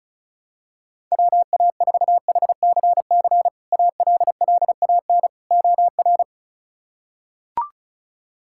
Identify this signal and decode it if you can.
Morse code, heard as a keyed tone, about 36 wpm, 700 Hz. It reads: WA4HCC ALLAN OR
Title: K1USN SST QSO Practice - 35wpm